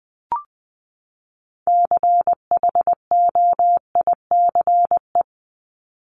Morse code, heard as a keyed tone, about 20 wpm, 700 Hz. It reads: CHOICE
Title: Top 751-1000 Words 20wpm